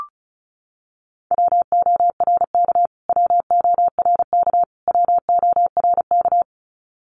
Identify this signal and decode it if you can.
Morse code, heard as a keyed tone, about 34 wpm, 700 Hz. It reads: WORK WORK WORK
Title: Top 500 Words Repeated 35wpm